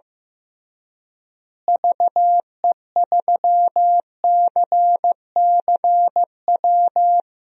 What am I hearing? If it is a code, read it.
Morse code, heard as a keyed tone, about 15 wpm, 700 Hz. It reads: VE3CCW
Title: DXCC-100 Call Signs - 6 Characters - Repeated 3x - 15wpm